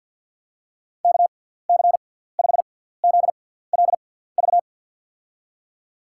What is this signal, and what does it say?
Morse code, heard as a keyed tone, about 47 wpm, spaced as for 17 wpm, 700 Hz. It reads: KC5ZLV
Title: USA Call Signs - 6 Characters - 6x Character Spacing - 50wpm